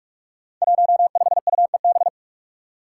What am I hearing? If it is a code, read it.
Morse code, heard as a keyed tone, about 44 wpm, 700 Hz. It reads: 15UEB
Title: Five Challenging Letter-Number Combinations 45wpm